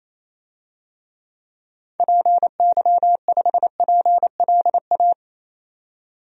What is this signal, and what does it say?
Morse code, heard as a keyed tone, about 28 wpm, 700 Hz. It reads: PY5PLA
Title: DXCC-100 Call Signs - 6 Characters - 28wpm